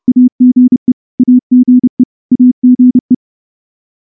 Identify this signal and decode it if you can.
Morse code, heard as a keyed tone, about 28 wpm, 250 Hz. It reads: AGE AGE AGE
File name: morse-training-30WPM